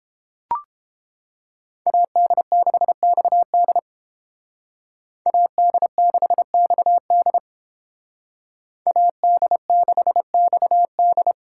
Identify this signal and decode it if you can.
Morse code, heard as a keyed tone, about 31 wpm, 700 Hz. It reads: AD6XD AD6XD AD6XD